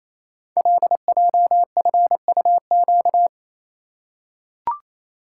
Morse code, heard as a keyed tone, about 29 wpm, 700 Hz. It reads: LJFUQ
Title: Code Groups - 5 Letters - 28wpm